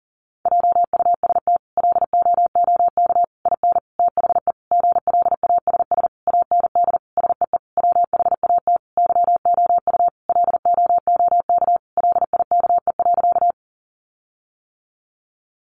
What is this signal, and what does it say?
Morse code, heard as a keyed tone, about 40 wpm, 700 Hz. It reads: JUST LOOK IN THE GLASS AND SEE WHAT YOU LOOK LIKE.
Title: Sentences from Top 1300 Words 40wpm